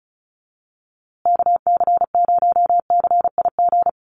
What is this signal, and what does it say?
Morse code, heard as a keyed tone, about 34 wpm, 700 Hz. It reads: KC0CIG
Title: USA Call Signs Repeated 2x 35wpm